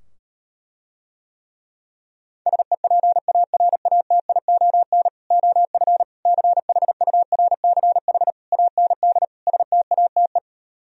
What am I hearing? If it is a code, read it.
Morse code, heard as a keyed tone, about 38 wpm, 700 Hz. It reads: SEPARATION OF CHURCH AND STATE